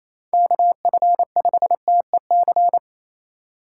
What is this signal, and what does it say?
Morse code, heard as a keyed tone, about 28 wpm, 700 Hz. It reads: KF5TEC
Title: USA Call Signs - 4x Character Spacing - 28wpm